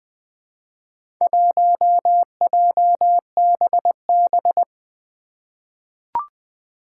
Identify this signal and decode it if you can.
Morse code, heard as a keyed tone, about 21 wpm, 700 Hz. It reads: 1JBB
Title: Four Challenging Letter-Number Combinations 20wpm